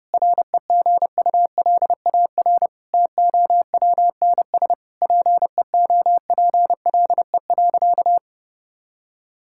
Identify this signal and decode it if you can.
Morse code, heard as a keyed tone, about 30 wpm, 700 Hz. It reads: REGULAR TOWNS PEOPLE.
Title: Sentences from Top 1000 Words - Review Part 6 - 30wpm